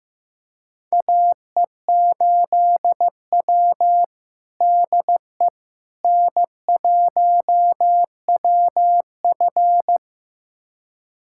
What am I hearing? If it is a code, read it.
Morse code, heard as a keyed tone, about 15 wpm, 700 Hz. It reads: AE8W DE N1WF
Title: USA Callsign DE Callsign - 4 Characters - 15wpm